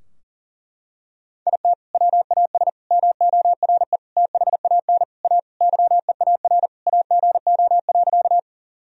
Morse code, heard as a keyed tone, about 37 wpm, 700 Hz. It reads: IT WAS MORE THAN A YEAR AGO.
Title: Sentences from Top 1000 Words - Review Part 3 - 40wpm